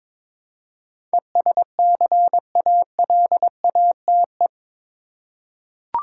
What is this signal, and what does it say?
Morse code, heard as a keyed tone, about 23 wpm, 700 Hz. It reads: ESCALATE
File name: practice / 22wpm / Eight Letter Words - 22wpm